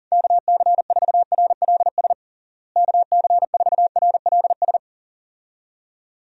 Morse code, heard as a keyed tone, about 37 wpm, 700 Hz. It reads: KC4RLS KC4RLS
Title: USA Call Signs - 6 Characters - Repeated 2x - 40wpm